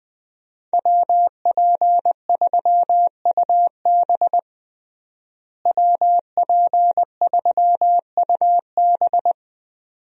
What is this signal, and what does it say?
Morse code, heard as a keyed tone, about 20 wpm, 700 Hz. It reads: WP3UB WP3UB
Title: USA Call Signs - 5 Characters - Repeated 2x - 20wpm